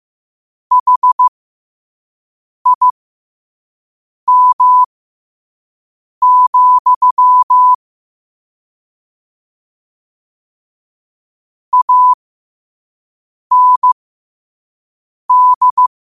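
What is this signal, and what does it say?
Morse code, heard as a keyed tone, about 15 wpm, spaced as for 5 wpm, 1000 Hz. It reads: HIM, AND
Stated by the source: Morse code quote of the day sent at 5 wpm.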